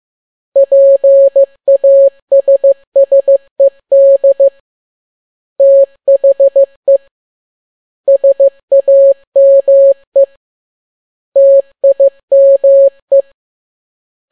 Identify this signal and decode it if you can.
Morse code, heard as a keyed tone, about 15 wpm, 550 Hz. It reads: PASSED THE SAME TIME
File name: n6tt p1 15C13